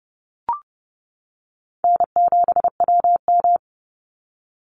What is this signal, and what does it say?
Morse code, heard as a keyed tone, about 30 wpm, 700 Hz. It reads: N7WM
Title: NA SOTA Chaser Callsigns 30wpm